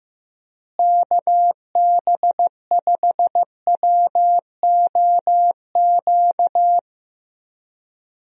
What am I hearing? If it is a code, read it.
Morse code, heard as a keyed tone, about 15 wpm, 700 Hz. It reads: KB5WOQ
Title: USA Call Signs - 6 Characters - Repeated 3x - 15wpm